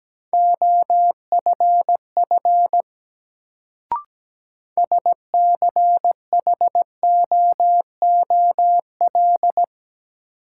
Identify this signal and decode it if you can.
Morse code, heard as a keyed tone, about 17 wpm, 700 Hz. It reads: OFF SCHOOL
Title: Sets of 2 Words - New Word Drill - 17wpm